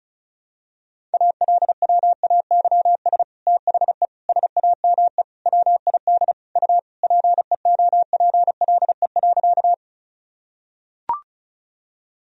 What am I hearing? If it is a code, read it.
Morse code, heard as a keyed tone, about 36 wpm, 700 Hz. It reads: ALWAYS THE SAME WID U PEOPLE.
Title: Sentences from Top 300 Words and Abbreviations 35wpm